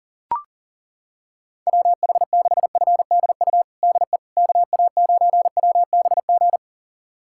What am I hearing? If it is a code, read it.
Morse code, heard as a keyed tone, about 40 wpm, 700 Hz. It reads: WH6FDU DE KA9WBG
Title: USA Callsign DE Callsign - 6 Characters - 40wpm